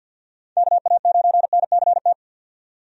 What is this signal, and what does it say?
Morse code, heard as a keyed tone, about 51 wpm, 700 Hz. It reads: KA9NCT
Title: USA Call Signs - 6 Characters - 50wpm